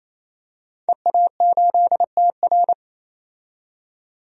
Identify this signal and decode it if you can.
Morse code, heard as a keyed tone, about 28 wpm, 700 Hz. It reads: EA8TR
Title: DXCC-100 Call Signs - 5 Characters - 28wpm